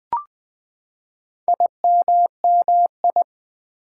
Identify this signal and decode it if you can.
Morse code, heard as a keyed tone, about 20 wpm, 700 Hz. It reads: IMMI
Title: Build A Word - 4 Letters At A Time - 11-Letter Words - 20wpm